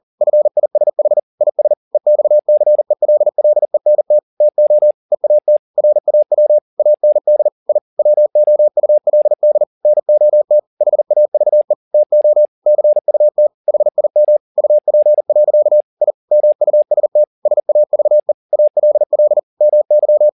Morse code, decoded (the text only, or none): FISH IS EXCELLENT TO EAT RAW AND I WOULD NOT HAVE TO CUT HIM UP. I MUST SAVE ALL MY